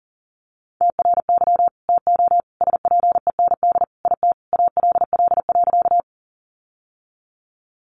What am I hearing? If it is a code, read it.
Morse code, heard as a keyed tone, about 40 wpm, 700 Hz. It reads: TRY TO SPEND IT ALL.
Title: Sentences from Top 800 Words 40wpm